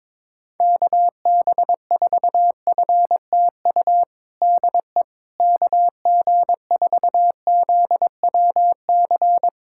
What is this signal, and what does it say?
Morse code, heard as a keyed tone, about 22 wpm, 700 Hz. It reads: KB4FTU DE KG4ZWC
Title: USA Callsign DE Callsign - 6 Characters - 22wpm